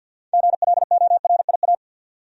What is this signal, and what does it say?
Morse code, heard as a keyed tone, about 50 wpm, 700 Hz. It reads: GLORIA
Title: USA Common Names - 50wpm